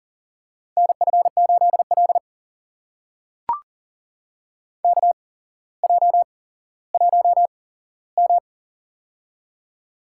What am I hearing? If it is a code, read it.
Morse code, heard as a keyed tone, about 43 wpm, spaced as for 17 wpm, 700 Hz. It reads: NF8L KJ1M
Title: USA Call Signs - 4 Characters - 8x Character Spacing - 40wpm